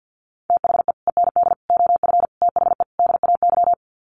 Morse code, heard as a keyed tone, about 50 wpm, 700 Hz. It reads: THE END OF THE DAY
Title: Sets of 5 Words 50wpm